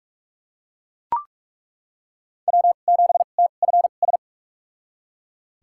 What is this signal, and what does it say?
Morse code, heard as a keyed tone, about 41 wpm, spaced as for 31 wpm, 700 Hz. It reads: W7TFS
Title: USA Call Signs - 2x Character Spacing - 45wpm